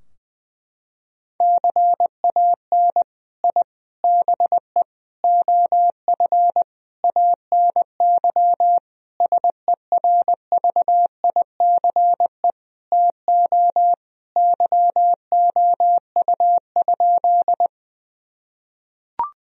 Morse code, heard as a keyed tone, about 20 wpm, 700 Hz. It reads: CAN I BE OF ANY SERVICE TO YOU?
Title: Sentences from Top 1000 Words - Review Part 4 - 20wpm